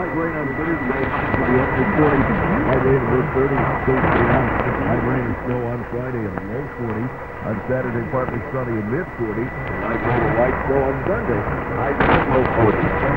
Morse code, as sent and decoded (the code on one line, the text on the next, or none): none